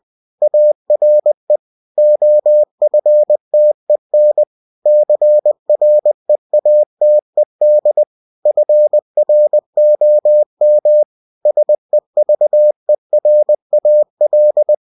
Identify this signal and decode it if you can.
Morse code, heard as a keyed tone, about 20 wpm, 600 Hz. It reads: ARE OFTEN CREATED FROM SEVERAL
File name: cw-books / winnie-the-pooh / winnie-20wpm-page-0122